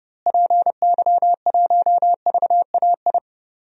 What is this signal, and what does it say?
Morse code, heard as a keyed tone, about 30 wpm, 700 Hz. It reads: PY1VAI
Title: DXCC-100 Call Signs - 6 Characters - Repeated 2x - 30wpm